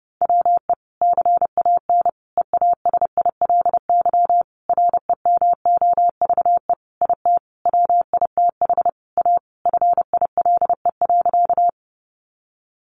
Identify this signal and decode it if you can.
Morse code, heard as a keyed tone, about 28 wpm, 700 Hz. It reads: WE CAN EASILY REMOVE IT WITH A FILE.
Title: Sentences from Top 1200 Words 30wpm